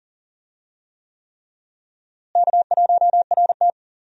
Morse code, heard as a keyed tone, about 39 wpm, 700 Hz. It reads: K1RT
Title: USA Call Signs - 4 Characters - Repeated 3x - 40wpm